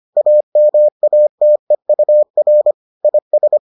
Morse code, heard as a keyed tone, about 25 wpm, 600 Hz. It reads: AMATEUR IS